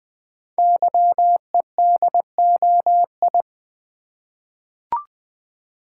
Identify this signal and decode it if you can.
Morse code, heard as a keyed tone, about 20 wpm, 700 Hz. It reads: YEDOI
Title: Code Groups - 5 Letters - 20wpm